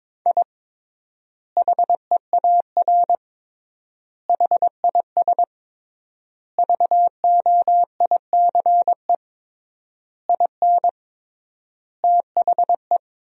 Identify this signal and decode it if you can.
Morse code, heard as a keyed tone, about 22 wpm, 700 Hz. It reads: I HEAR HIS VOICE IN THE
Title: 3x Word-Spacing - Sentences from Top 500 Words - 22wpm